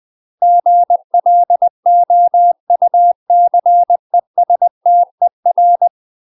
Morse code, heard as a keyed tone, about 20 wpm, 700 Hz. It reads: GLOUCESTER
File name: UK MASTER QSOs TRAINING / 20WPM / UK MASTER QSOs 0022